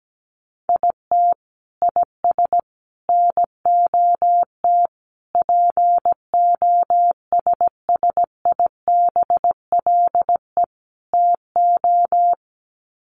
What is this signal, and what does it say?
Morse code, heard as a keyed tone, about 17 wpm, 700 Hz. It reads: IT IS NOT POSSIBLE TO
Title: Sets of 5 Words Encore2 17wpm